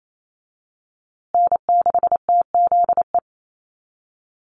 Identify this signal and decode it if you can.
Morse code, heard as a keyed tone, about 28 wpm, 700 Hz. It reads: N6TZE
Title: USA Call Signs Repeated 2x 28wpm